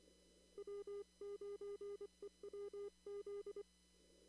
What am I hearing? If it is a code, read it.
Morse code, heard as a keyed tone, about 25 wpm, 400 Hz. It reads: W9EWZ
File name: CW / kt5j wpx cw-op-k5pi